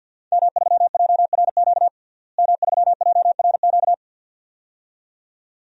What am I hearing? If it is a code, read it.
Morse code, heard as a keyed tone, about 50 wpm, 700 Hz. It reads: M3JRQ M3JRQ
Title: DXCC-100 Call Signs - 5 Characters - Repeated 2x - 50wpm